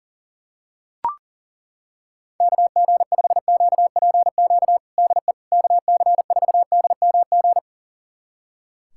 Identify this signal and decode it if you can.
Morse code, heard as a keyed tone, about 40 wpm, 700 Hz. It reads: KG5QPQ DE KC4DMG
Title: USA Callsign DE Callsign - 6 Characters - 40wpm